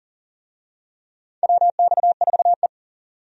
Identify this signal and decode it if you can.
Morse code, heard as a keyed tone, about 40 wpm, 700 Hz. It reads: WX4E
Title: USA Call Signs - 4 Characters - 40wpm